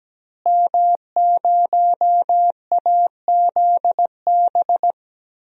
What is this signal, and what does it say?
Morse code, heard as a keyed tone, about 17 wpm, 700 Hz. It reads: M0AZB
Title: DXCC-100 Call Signs - 5 Characters - Repeated 2x - 17wpm